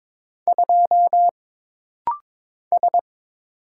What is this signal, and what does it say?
Morse code, heard as a keyed tone, about 22 wpm, 700 Hz. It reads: 2 S
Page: This rapid-fire timing is geared to push your ICR (Instant Character Recognition) skills.